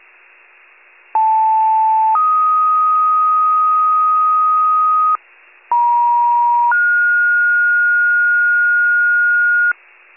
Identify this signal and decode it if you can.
Two-tone page, heard as a one-second tone, then a three-second tone, about 880 then 1250 Hz, about 950 then 1430 Hz. It is heard from 1.1 to 9.7 s.